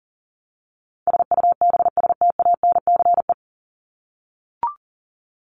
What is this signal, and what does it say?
Morse code, heard as a keyed tone, about 40 wpm, 700 Hz. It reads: SUBSTANCE
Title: Top 1251-1500 Words 40wpm